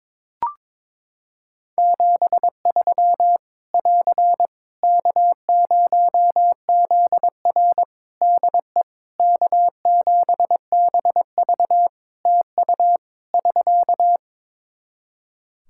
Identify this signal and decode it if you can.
Morse code, heard as a keyed tone, about 22 wpm, 700 Hz. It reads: 73 <AR> K0ZR DE K7BV TU <SK>
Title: K1USN SST QSO Practice 22wpm